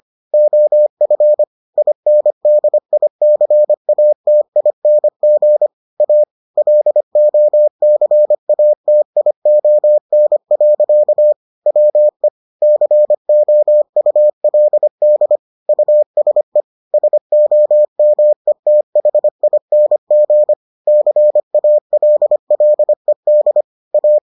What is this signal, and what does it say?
Morse code, heard as a keyed tone, about 25 wpm, 600 Hz. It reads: OF INDICATING A LOCATION. WE COULD USE SOMETHING CALLED A